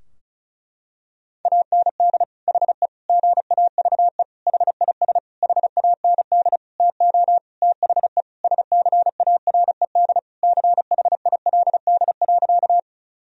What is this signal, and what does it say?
Morse code, heard as a keyed tone, about 34 wpm, 700 Hz. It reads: AND HE GAVE HIS HAND TO THE SCARED CHILD.
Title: Sentences from Top 1000 Words - Review Part 4 - 35wpm